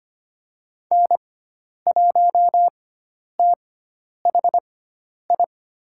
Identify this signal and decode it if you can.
Morse code, heard as a keyed tone, about 25 wpm, spaced as for 10 wpm, 700 Hz. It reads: N1THI
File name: Lesson 16g - Letter c - Call Signs 25wpm